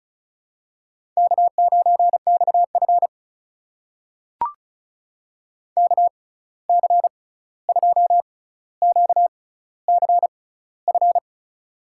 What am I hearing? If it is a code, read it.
Morse code, heard as a keyed tone, about 36 wpm, spaced as for 16 wpm, 700 Hz. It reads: K9XF KC2QCF
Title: USA Call Signs - 6x Character Spacing - 35wpm